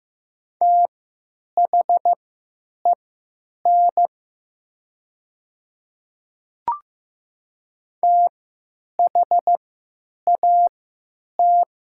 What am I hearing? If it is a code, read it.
Morse code, heard as a keyed tone, about 15 wpm, spaced as for 9 wpm, 700 Hz. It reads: THEN THAT
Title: 3x Character Spacing - Top 100 Words - 15wpm